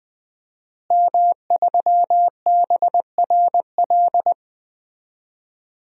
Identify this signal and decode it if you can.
Morse code, heard as a keyed tone, about 20 wpm, 700 Hz. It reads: M3BRL
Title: DXCC-100 Call Signs - 5 Characters - Repeated 3x - 20wpm